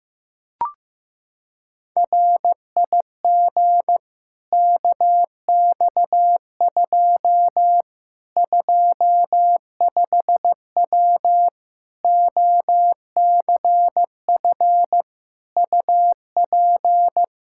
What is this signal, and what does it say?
Morse code, heard as a keyed tone, about 15 wpm, 700 Hz. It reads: RIG KX2 25W OCF UP
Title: K1USN SST QSO Practice 15wpm